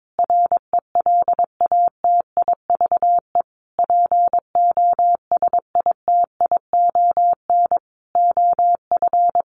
Morse code, heard as a keyed tone, about 22 wpm, 700 Hz. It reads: RELATIVE POSITION OF
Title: Sentences from Top 1400 Words 22wpm